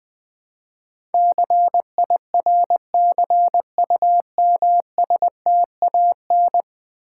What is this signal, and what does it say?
Morse code, heard as a keyed tone, about 20 wpm, 700 Hz. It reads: CIRCUMSTAN
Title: Build A Word - 5 Letters At A Time - 12-Letter Words - 20wpm